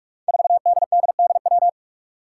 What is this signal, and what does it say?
Morse code, heard as a keyed tone, about 41 wpm, 700 Hz. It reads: 4DDDW
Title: Five Challenging Letter-Number Combinations 45wpm